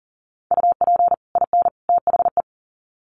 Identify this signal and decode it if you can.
Morse code, heard as a keyed tone, about 40 wpm, 700 Hz. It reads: UP IN THE
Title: Sets of 3 Words 40wpm